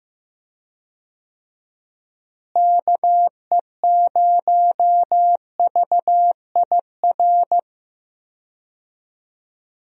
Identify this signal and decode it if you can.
Morse code, heard as a keyed tone, about 15 wpm, 700 Hz. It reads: KE0VIR
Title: USA Call Signs - 6 Characters - Repeated 3x - 15wpm